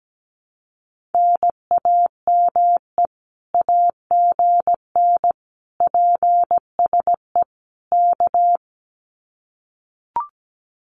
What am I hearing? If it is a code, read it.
Morse code, heard as a keyed tone, about 17 wpm, 700 Hz. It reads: NAME AGN PSE K
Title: Common QSO Phrases Encore 17wpm